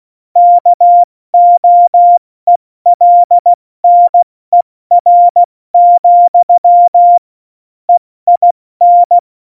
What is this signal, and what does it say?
Morse code, heard as a keyed tone, about 16 wpm, 700 Hz. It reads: KOELNER, EIN
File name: Morsekurs / Witze / Witz 08. Oktober 2019, 70 F80